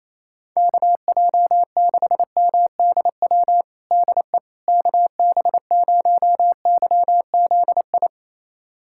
Morse code, heard as a keyed tone, about 28 wpm, 700 Hz. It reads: KJ6MDW DE KB0YZI
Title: USA Callsign DE Callsign 28wpm